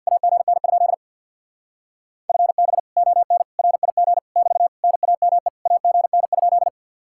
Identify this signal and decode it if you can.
Morse code, heard as a keyed tone, about 51 wpm, 700 Hz. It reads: AGN? FB ON RIG <BT> NAME AGN?
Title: Speed-Racing Common QSO Phrases - 50wpm